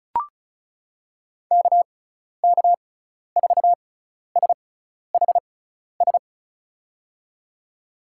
Morse code, heard as a keyed tone, about 35 wpm, spaced as for 12 wpm, 690 Hz. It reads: KK4SHS
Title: USA Call Signs - 6 Characters - 6x Character Spacing - 35wpm